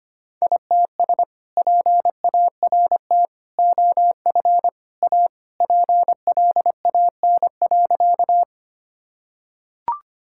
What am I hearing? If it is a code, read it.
Morse code, heard as a keyed tone, about 25 wpm, 700 Hz. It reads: ITS PART OF A PLAN.
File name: practice / 25wpm / Sentences with 3-Word Phrases - 25wpm